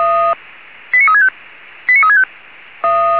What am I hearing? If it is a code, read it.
Five-tone selective call (ZVEI), heard as quick runs of five tones, tones of about 70 ms. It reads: 89365 89365
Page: SDR/ZVEI